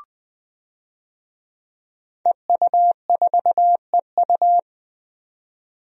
Morse code, heard as a keyed tone, about 20 wpm, 700 Hz. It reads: EU4EU
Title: DXCC-100 Call Signs - 5 Characters - Repeated 2x - 20wpm